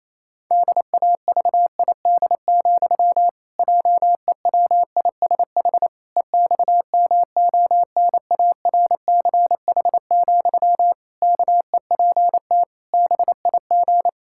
Morse code, heard as a keyed tone, about 28 wpm, 700 Hz. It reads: DAVID, JEWISH EXMONARCH, KEPT BIG
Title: Pangram Challenge v2 - 28wpm